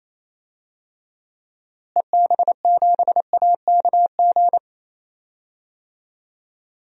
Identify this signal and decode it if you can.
Morse code, heard as a keyed tone, about 28 wpm, 700 Hz. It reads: EB7AKG
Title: DXCC-100 Call Signs - 6 Characters - Repeated 3x - 28wpm